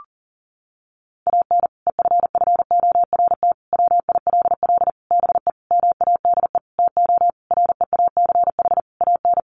Morse code, heard as a keyed tone, about 37 wpm, 700 Hz. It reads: AN EFFORT WILL BE MADE TO REACH AN
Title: Sentences from Top 1500 Words 40wpm